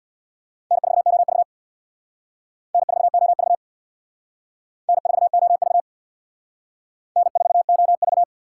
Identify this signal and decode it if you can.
Morse code, heard as a keyed tone, about 60 wpm, 700 Hz. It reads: N4OV N4OV N4OV N4OV
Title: Speed Racing - USA Call Signs 50wpm